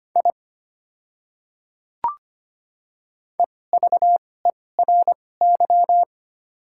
Morse code, heard as a keyed tone, about 25 wpm, spaced as for 18 wpm, 700 Hz. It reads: I EVERY
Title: 2x Character Spacing - Top 200 Words - 25wpm